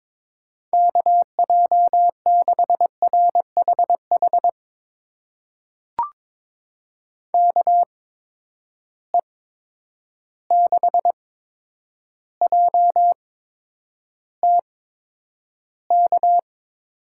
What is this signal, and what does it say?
Morse code, heard as a keyed tone, about 23 wpm, spaced as for 10 wpm, 700 Hz. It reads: KJ6RHH KE6JTK
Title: USA Call Signs - 6 Characters - 8x Character Spacing - 22wpm